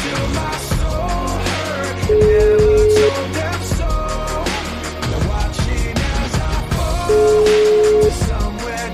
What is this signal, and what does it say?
Ringback tone (Europe, 425 Hz), with noise.